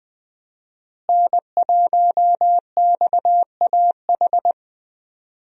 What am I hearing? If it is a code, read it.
Morse code, heard as a keyed tone, about 20 wpm, 700 Hz. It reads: N1XAH
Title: USA Call Signs - 5 Characters - Repeated 2x - 20wpm